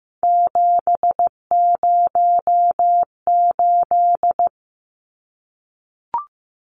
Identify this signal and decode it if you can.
Morse code, heard as a keyed tone, about 15 wpm, 700 Hz. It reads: 708
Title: Numbers - 3-Digit 15wpm